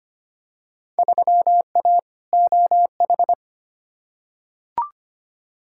Morse code, heard as a keyed tone, about 25 wpm, 700 Hz. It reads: 3A OH
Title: Field Day Exchanges - 25wpm